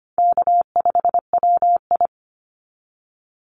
Morse code, heard as a keyed tone, about 25 wpm, 700 Hz. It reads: K5WI
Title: NA SOTA Chaser Callsigns 25wpm